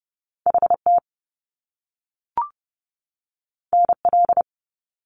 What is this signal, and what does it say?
Morse code, heard as a keyed tone, about 30 wpm, 700 Hz. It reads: HT NL
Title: Top Two Letter Combinations in Words 30wpm